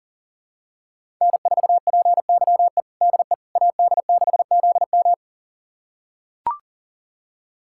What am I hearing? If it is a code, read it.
Morse code, heard as a keyed tone, about 37 wpm, 700 Hz. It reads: N4PYE DE AD6ZM
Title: USA Callsign DE Callsign - 5 Characters - 40wpm